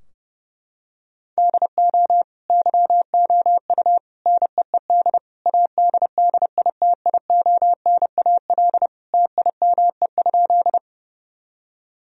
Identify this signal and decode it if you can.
Morse code, heard as a keyed tone, about 30 wpm, 700 Hz. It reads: DO YOU NEED ADDITIONAL TIME?
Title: Sentences from Top 1000 Words - Review Part 4 - 30wpm